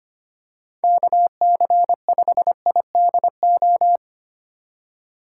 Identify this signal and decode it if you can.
Morse code, heard as a keyed tone, about 25 wpm, 700 Hz. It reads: KC5IDO
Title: USA Call Signs - 2x Character Spacing - 25wpm